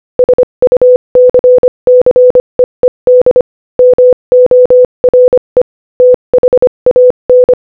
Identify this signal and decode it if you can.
Morse code, heard as a keyed tone, about 25 wpm, 500 Hz. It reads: SUCCEED MORE THAN
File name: Morse-Code-Audio-Speed-25-13